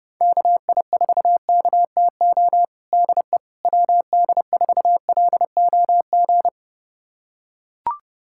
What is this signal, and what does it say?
Morse code, heard as a keyed tone, about 32 wpm, 700 Hz. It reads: KI4KTO DE WD4LOG
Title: USA Callsign DE Callsign - 6 Characters - 30wpm